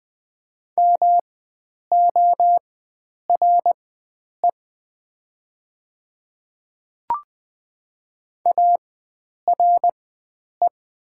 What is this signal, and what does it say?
Morse code, heard as a keyed tone, about 20 wpm, spaced as for 9 wpm, 700 Hz. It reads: MORE ARE
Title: 4x Character Spacing - Top 100 Words - 20wpm